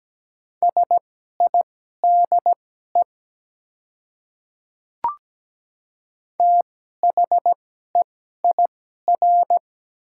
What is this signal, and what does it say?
Morse code, heard as a keyed tone, about 17 wpm, spaced as for 12 wpm, 700 Hz. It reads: SIDE THEIR
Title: 2x Character Spacing - Top 100 Words - 17wpm